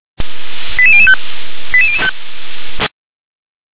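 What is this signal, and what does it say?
Five-tone selective call (ZVEI), heard as quick runs of five tones, tones of about 70 ms. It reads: 88005 88005
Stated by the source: ZVEI five-tone selective call received via SDR.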